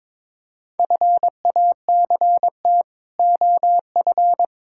Morse code, heard as a keyed tone, about 21 wpm, 700 Hz. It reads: FACT OF
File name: practice / 22wpm / Build A Phrase - 5-Word Phrases - 22wpm